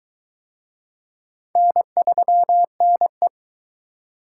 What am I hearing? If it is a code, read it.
Morse code, heard as a keyed tone, about 23 wpm, 700 Hz. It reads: N3NE